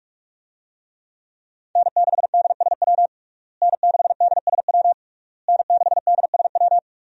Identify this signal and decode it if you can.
Morse code, heard as a keyed tone, about 41 wpm, 700 Hz. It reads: N6DSW N6DSW N6DSW
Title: USA Call Signs - 5 Characters - Repeated 3x - 45wpm